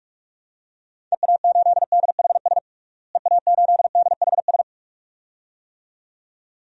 Morse code, heard as a keyed tone, about 46 wpm, 700 Hz. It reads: EA8DHS EA8DHS
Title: DXCC-100 Call Signs - 6 Characters - Repeated 2x - 45wpm